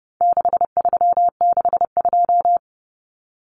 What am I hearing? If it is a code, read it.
Morse code, heard as a keyed tone, about 30 wpm, 700 Hz. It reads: 6362
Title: Numbers - 4-Digit 30wpm